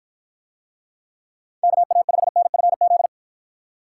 Morse code, heard as a keyed tone, about 49 wpm, 700 Hz. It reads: KA5NFZ